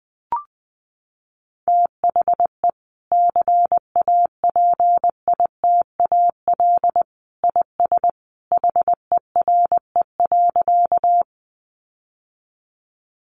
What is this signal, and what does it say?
Morse code, heard as a keyed tone, about 20 wpm, 700 Hz. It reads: THE CAPITAL IS HERE.
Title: Sentences from Top 1000 Words 20wpm